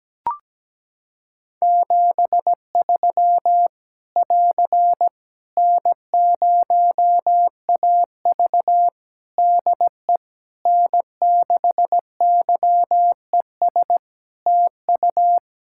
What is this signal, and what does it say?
Morse code, heard as a keyed tone, about 17 wpm, 700 Hz. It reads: 73 <AR> N0AV DE N6YES TU
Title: Practice QSO - Exchange 4 - Ending - 17wpm